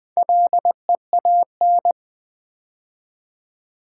Morse code, heard as a keyed tone, about 20 wpm, 700 Hz. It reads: LEAN
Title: Build A Word - 2 Letters At A Time - 4-Letter Words - 20wpm